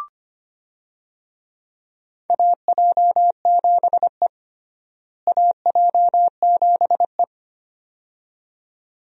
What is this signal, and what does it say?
Morse code, heard as a keyed tone, about 25 wpm, 700 Hz. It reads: AJ7E AJ7E
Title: USA Call Signs - 4 Characters - Repeated 2x - 25wpm